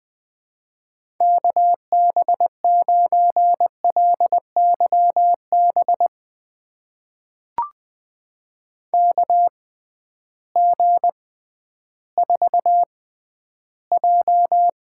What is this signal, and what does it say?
Morse code, heard as a keyed tone, about 20 wpm, spaced as for 12 wpm, 700 Hz. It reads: KB9LYB KG4J
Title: USA Call Signs - 6 Characters - 6x Character Spacing - 20wpm